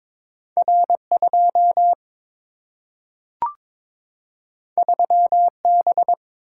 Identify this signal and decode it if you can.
Morse code, heard as a keyed tone, about 22 wpm, 700 Hz. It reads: R 2 3 B
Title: Single-Letter Single-Number 22wpm